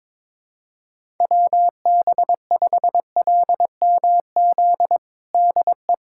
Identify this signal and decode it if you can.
Morse code, heard as a keyed tone, about 22 wpm, 700 Hz. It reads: WB5LMZ DE
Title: USA Callsign DE Callsign 22wpm